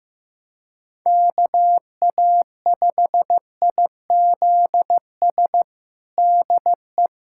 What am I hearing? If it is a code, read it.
Morse code, heard as a keyed tone, about 15 wpm, 700 Hz. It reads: KA5IZS DE
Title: USA Callsign DE Callsign - 6 Characters - 15wpm